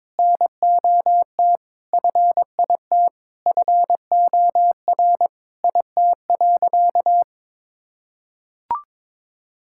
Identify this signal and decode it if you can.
Morse code, heard as a keyed tone, about 22 wpm, 700 Hz. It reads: NOT FIT FOR IT.
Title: Sentences from Top 1000 Words - Review Part 5 - 22wpm